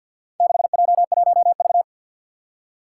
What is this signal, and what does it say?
Morse code, heard as a keyed tone, about 52 wpm, 700 Hz. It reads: 6J1V
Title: Four Challenging Letter-Number Combinations 50wpm